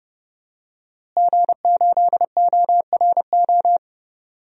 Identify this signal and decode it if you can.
Morse code, heard as a keyed tone, about 32 wpm, 700 Hz. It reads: G8ORO
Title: DXCC-100 Call Signs - 5 Characters - Repeated 3x - 30wpm